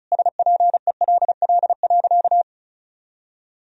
Morse code, heard as a keyed tone, about 34 wpm, 700 Hz. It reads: SPELL.
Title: Sentences from Top 1000 Words - Review Part 1 - 35wpm